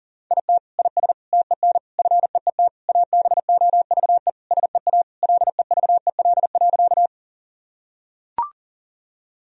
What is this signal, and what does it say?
Morse code, heard as a keyed tone, about 40 wpm, 700 Hz. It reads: IT IS TEN FEET ABOVE SEA LEVEL.
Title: Sentences from Top 1000 Words - Review Part 4 - 40wpm